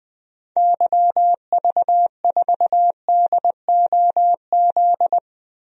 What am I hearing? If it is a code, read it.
Morse code, heard as a keyed tone, about 21 wpm, 700 Hz. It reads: YV4DOZ
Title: DXCC-100 Call Signs - 6 Characters - 20wpm